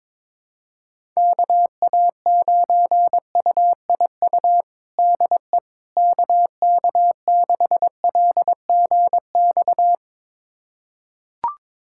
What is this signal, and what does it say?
Morse code, heard as a keyed tone, about 22 wpm, 700 Hz. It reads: KA9UIU DE KK6LGX
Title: USA Callsign DE Callsign - 6 Characters - 22wpm